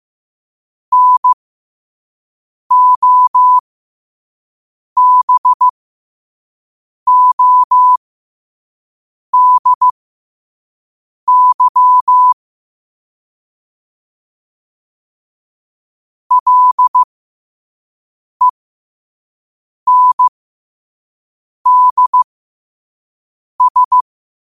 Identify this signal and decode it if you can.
Morse code, heard as a keyed tone, about 15 wpm, spaced as for 5 wpm, 1000 Hz. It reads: NOBODY LENDS
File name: qotdfiles / 5wpm